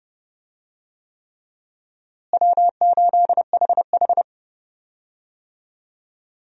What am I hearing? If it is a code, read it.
Morse code, heard as a keyed tone, about 32 wpm, 700 Hz. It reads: W8HH
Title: USA Call Signs - 4 Characters - Repeated 3x - 30wpm